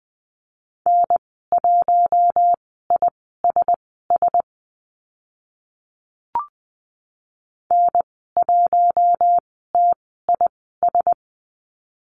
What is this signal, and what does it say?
Morse code, heard as a keyed tone, about 20 wpm, spaced as for 14 wpm, 700 Hz. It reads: N1ISS N1TIS
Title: 042 Lesson 8g - Number 1 - Call Signs 20wpm